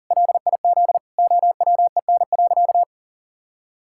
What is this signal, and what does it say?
Morse code, heard as a keyed tone, about 40 wpm, 700 Hz. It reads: LIZ OWEN.
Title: Pangram Challenge v2 - 40wpm